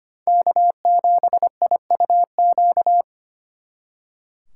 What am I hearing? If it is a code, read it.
Morse code, heard as a keyed tone, about 30 wpm, 700 Hz. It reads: K7IUQ
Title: Speed Racing - USA 1x3 Call Signs 25wpm